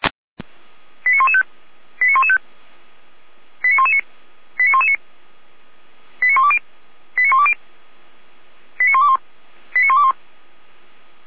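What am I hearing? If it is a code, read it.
Five-tone selective call (ZVEI), heard as quick runs of five tones, tones of about 70 ms. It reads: 89115 89115 89119 89119 89120 89120 89121 89121
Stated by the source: Received on an SDR; ZVEI five-tone alert from the server of Feuerwehr Orlamünde (f85360).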